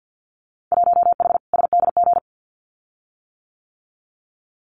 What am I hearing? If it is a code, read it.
Morse code, heard as a keyed tone, about 50 wpm, 700 Hz. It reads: 1H SDG
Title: Winter Field Day Exchanges 50wpm